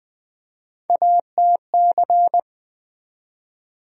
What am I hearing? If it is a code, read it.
Morse code, heard as a keyed tone, about 20 wpm, 700 Hz. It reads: ATC
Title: USA Call Sign Postfixes - 3 Letters 20wpm